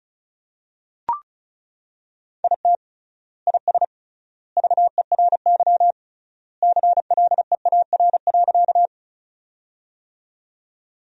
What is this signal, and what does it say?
Morse code, heard as a keyed tone, about 34 wpm, 700 Hz. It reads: IT IS VERY CLEAR.
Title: 3x Word-Spacing - Sentences from Top 500 Words - 35wpm